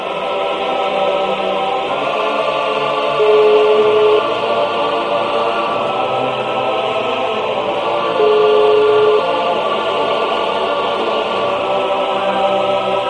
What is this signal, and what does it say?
Ringback tone (Europe, 425 Hz), with noise.